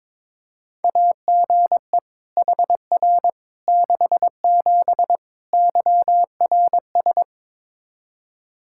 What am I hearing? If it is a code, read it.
Morse code, heard as a keyed tone, about 22 wpm, 700 Hz. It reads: AGE HR 67 YRS
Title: Practice QSO - Exchange 3 - More Info - 22wpm